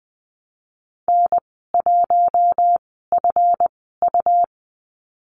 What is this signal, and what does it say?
Morse code, heard as a keyed tone, about 20 wpm, spaced as for 15 wpm, 700 Hz. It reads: N1FU
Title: 179 Lesson 29g - Slash - Call Signs 20wpm